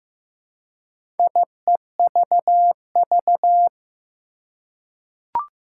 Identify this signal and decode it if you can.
Morse code, heard as a keyed tone, about 15 wpm, 700 Hz. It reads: IEVV
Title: Four Challenging Letter-Number Combinations 15wpm